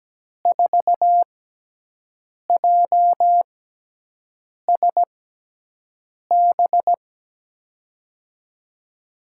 Morse code, heard as a keyed tone, about 17 wpm, spaced as for 6 wpm, 700 Hz. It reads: 4JSB
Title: USA Call Signs - 6 Characters - 6x Character Spacing - 17wpm